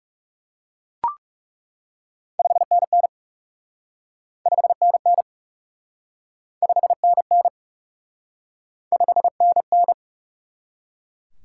Morse code, heard as a keyed tone, about 38 wpm, 700 Hz. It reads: 5NN 5NN 5NN 5NN